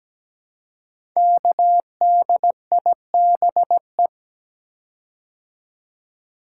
Morse code, heard as a keyed tone, about 17 wpm, 700 Hz. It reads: KDIBE
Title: Code Groups - 5 Letters - 17wpm